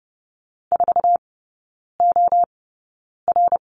Morse code, heard as a keyed tone, about 30 wpm, spaced as for 9 wpm, 700 Hz. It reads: 4OR
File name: 081 Lesson 14e - Number 2 - Three Characters 30wpm